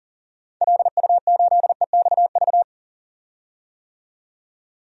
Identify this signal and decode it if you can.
Morse code, heard as a keyed tone, about 38 wpm, 700 Hz. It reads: LU8EXV
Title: DXCC-100 Call Signs - 6 Characters - 40wpm